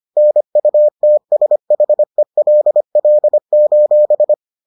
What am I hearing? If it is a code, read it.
Morse code, heard as a keyed tone, about 25 wpm, 600 Hz. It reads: NUTSHELL: